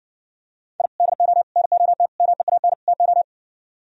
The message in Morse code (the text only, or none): I DO NOT MEAN TO